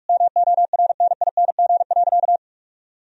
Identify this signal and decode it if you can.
Morse code, heard as a keyed tone, about 46 wpm, 700 Hz. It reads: MORNING.
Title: Sentences from Top 1000 Words - Review Part 2 - 45wpm